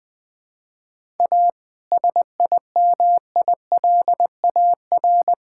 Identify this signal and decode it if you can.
Morse code, heard as a keyed tone, about 19 wpm, 700 Hz. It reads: A SIMILAR
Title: Sentences from Top 1000 Words - Review Part 5 - 20wpm